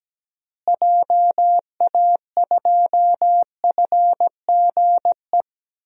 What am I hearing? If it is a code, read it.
Morse code, heard as a keyed tone, about 17 wpm, 700 Hz. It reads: JA2FGE
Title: DXCC-100 Call Signs - 6 Characters - Repeated 2x - 17wpm